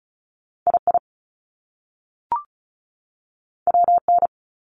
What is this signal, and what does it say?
Morse code, heard as a keyed tone, about 35 wpm, 700 Hz. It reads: I I W N
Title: Top Two Letter Combinations in Words 35wpm